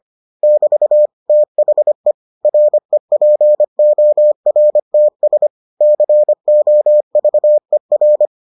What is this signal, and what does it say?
Morse code, heard as a keyed tone, about 25 wpm, 600 Hz. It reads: <BT> THE REPORTS COVER